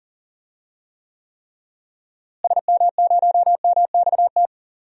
Morse code, heard as a keyed tone, about 40 wpm, 700 Hz. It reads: SM0MXT
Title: DXCC-100 Call Signs - 6 Characters - Repeated 3x - 40wpm